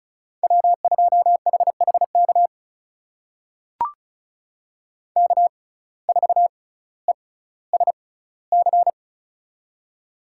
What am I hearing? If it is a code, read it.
Morse code, heard as a keyed tone, about 34 wpm, spaced as for 18 wpm, 700 Hz. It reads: W2HHK K4ESC
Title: USA Call Signs - 5 Characters - 6x Character Spacing - 35wpm